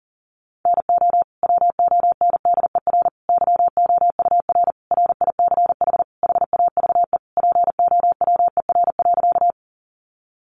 Morse code, heard as a keyed tone, about 40 wpm, 700 Hz. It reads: NO WONDER YOUR RICH HAVE POWER.
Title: Sentences from Top 1100 Words 40wpm